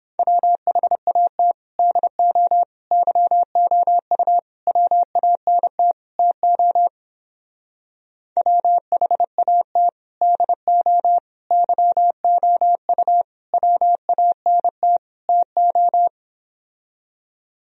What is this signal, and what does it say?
Morse code, heard as a keyed tone, about 29 wpm, 700 Hz. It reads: WHAT DO YOU WANT TO WHAT DO YOU WANT TO